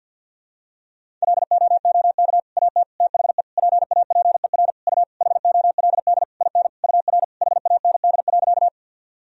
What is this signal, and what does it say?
Morse code, heard as a keyed tone, about 46 wpm, 700 Hz. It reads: LOOK AT THE PAPER U HOLD IN UR HAND.
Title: Sentences from Top 400 Words and Abbreviations 50wpm